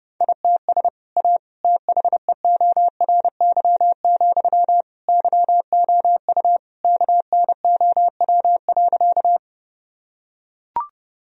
Morse code, heard as a keyed tone, about 30 wpm, 700 Hz. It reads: ITS A THEORY, YOU KNOW.
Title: Sentences from Top 1000 Words - Review Part 3 - 30wpm